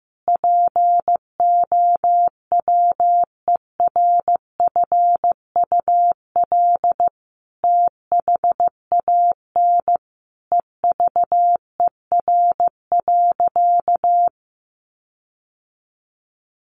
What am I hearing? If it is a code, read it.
Morse code, heard as a keyed tone, about 15 wpm, 700 Hz. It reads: POWERFUL THAN EVER.
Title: Sentences from Top 1300 Words 15wpm